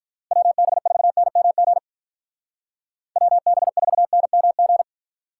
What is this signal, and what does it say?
Morse code, heard as a keyed tone, about 43 wpm, 700 Hz. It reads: WB4NMG WB4NMG
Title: Speed Racing - USA 2x3 Call Signs 35wpm